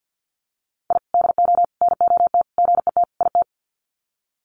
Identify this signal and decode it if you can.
Morse code, heard as a keyed tone, about 50 wpm, 700 Hz. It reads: I DO NOT GET IT
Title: Sets of 5 Words Encore 50wpm